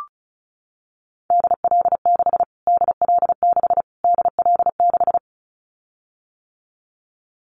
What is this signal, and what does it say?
Morse code, heard as a keyed tone, about 34 wpm, 700 Hz. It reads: DL6 DL6 DL6
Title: Callsign Prefixes for Top 25 DXCC - Repeated 3x - 35wpm